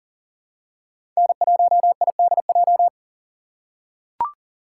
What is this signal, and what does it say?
Morse code, heard as a keyed tone, about 40 wpm, 700 Hz. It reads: N1IDJ
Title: USA Call Signs - 6x Character Spacing - 40wpm